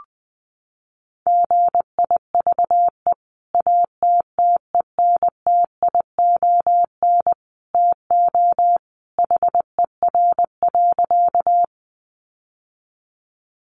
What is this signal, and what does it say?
Morse code, heard as a keyed tone, about 19 wpm, 700 Hz. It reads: GIVE ATTENTION TO HER.
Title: Sentences from Top 1000 Words 20wpm